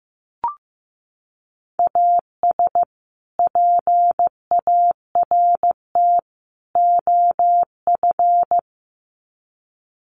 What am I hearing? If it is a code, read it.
Morse code, heard as a keyed tone, about 15 wpm, 700 Hz. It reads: AS PART OF
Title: Sets of 3 Words 15wpm